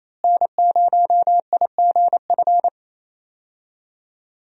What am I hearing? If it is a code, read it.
Morse code, heard as a keyed tone, about 28 wpm, 700 Hz. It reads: N0IGF
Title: USA Call Signs - 5 Characters - Repeated 3x - 28wpm